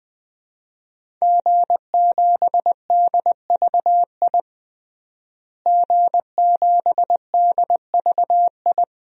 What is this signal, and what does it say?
Morse code, heard as a keyed tone, about 21 wpm, 700 Hz. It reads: G7DVI G7DVI
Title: DXCC-100 Call Signs - 5 Characters - Repeated 3x - 20wpm